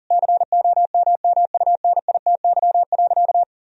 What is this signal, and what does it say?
Morse code, heard as a keyed tone, about 37 wpm, 700 Hz. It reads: COMMUNITY.
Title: Sentences from Top 1000 Words - Review Part 4 - 40wpm